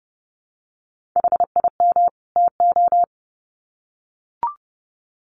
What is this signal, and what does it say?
Morse code, heard as a keyed tone, about 30 wpm, 700 Hz. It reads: HIM TO
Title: Sets of 2 Words 30wpm